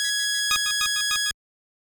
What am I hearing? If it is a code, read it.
There is no code.